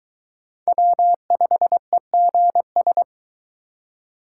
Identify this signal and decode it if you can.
Morse code, heard as a keyed tone, about 22 wpm, 700 Hz. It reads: W5EGS